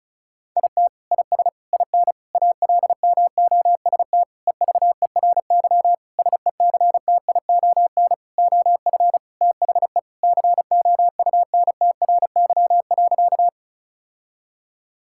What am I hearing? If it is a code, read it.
Morse code, heard as a keyed tone, about 34 wpm, 700 Hz. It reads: IT IS IN ALMOST EVERY SECTION OF THE COUNTRY.
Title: Sentences from Top 1000 Words - Review Part 5 - 35wpm